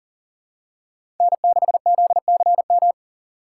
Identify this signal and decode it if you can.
Morse code, heard as a keyed tone, about 37 wpm, 700 Hz. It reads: N6ZCM
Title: USA Call Signs - 5 Characters - 2x Character Spacing - 40wpm